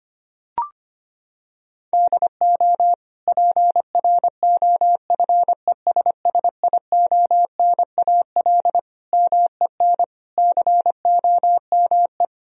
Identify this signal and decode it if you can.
Morse code, heard as a keyed tone, about 25 wpm, 700 Hz. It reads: DO PROFESSIONAL MEN COME
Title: Sentences from Top 1000 Words - Review Part 4 - 25wpm